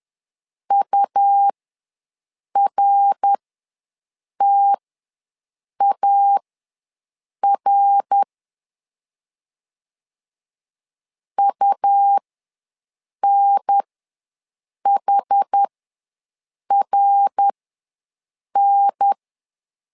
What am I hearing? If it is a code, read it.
Morse code, heard as a keyed tone, about 10 wpm, spaced as for 6 wpm, 800 Hz. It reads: URTAR UNHRN
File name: LEZIONI / 3 MWRO / 3 MWR ws 10wpm 2151